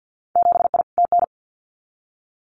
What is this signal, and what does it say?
Morse code, heard as a keyed tone, about 50 wpm, 700 Hz. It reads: 7I TN
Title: Winter Field Day Exchanges Encore 50wpm